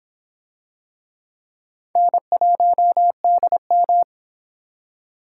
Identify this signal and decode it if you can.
Morse code, heard as a keyed tone, about 25 wpm, 700 Hz. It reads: N1DM